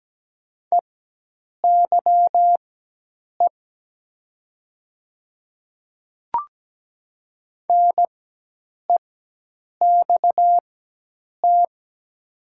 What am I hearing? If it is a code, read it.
Morse code, heard as a keyed tone, about 17 wpm, spaced as for 8 wpm, 700 Hz. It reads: EYE NEXT
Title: 4x Character Spacing - Top 300 Words - 17wpm